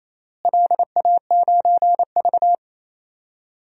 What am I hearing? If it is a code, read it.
Morse code, heard as a keyed tone, about 28 wpm, 700 Hz. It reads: LA9V
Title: DXCC-100 Call Signs - 4 Characters - Repeated 3x - 28wpm